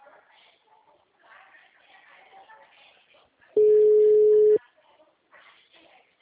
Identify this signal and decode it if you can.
Ringback tone (Europe, 425 Hz), with speech.